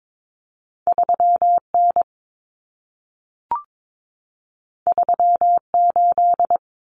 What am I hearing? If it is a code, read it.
Morse code, heard as a keyed tone, about 22 wpm, 700 Hz. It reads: 3N 38
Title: Common Cut Numbers - 2-Digit 22wpm